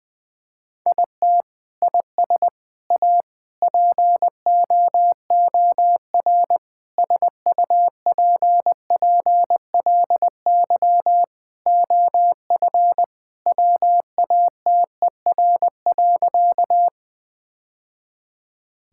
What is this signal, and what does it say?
Morse code, heard as a keyed tone, about 20 wpm, 700 Hz. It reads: IT IS A POOR SUPPLY OF WATER.
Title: Sentences from Top 1000 Words - Review Part 5 - 20wpm